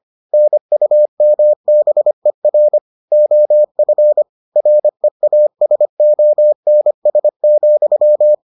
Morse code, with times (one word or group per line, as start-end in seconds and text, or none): 0.3-2.8 NUMBER
3.1-4.2 OF
4.6-8.3 REASONS,